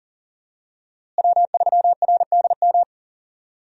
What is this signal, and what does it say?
Morse code, heard as a keyed tone, about 40 wpm, 700 Hz. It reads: W3RDM
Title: USA Call Signs - 5 Characters - Repeated 2x - 40wpm